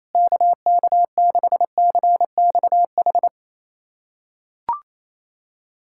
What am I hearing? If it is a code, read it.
Morse code, heard as a keyed tone, about 28 wpm, 700 Hz. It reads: KK6CXH
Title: USA Callsign DE Callsign 28wpm